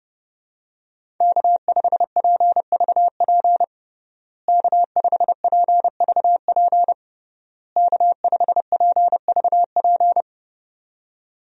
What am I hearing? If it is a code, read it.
Morse code, heard as a keyed tone, about 30 wpm, 700 Hz. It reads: K5PVP K5PVP K5PVP
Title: USA Call Signs - 5 Characters - Repeated 3x - 30wpm